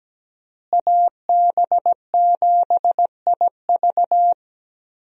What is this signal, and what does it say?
Morse code, heard as a keyed tone, about 17 wpm, 700 Hz. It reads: AB7IV
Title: USA Call Signs - 5 Characters - Repeated 3x - 17wpm